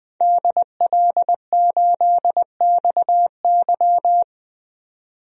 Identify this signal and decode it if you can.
Morse code, heard as a keyed tone, about 21 wpm, 700 Hz. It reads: DL8XY
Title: DXCC-100 Call Signs - 5 Characters - 20wpm